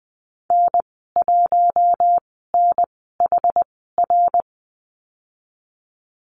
Morse code, heard as a keyed tone, about 20 wpm, spaced as for 14 wpm, 700 Hz. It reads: N1NHR
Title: 082 Lesson 14g - Number 2 - Call Signs 20wpm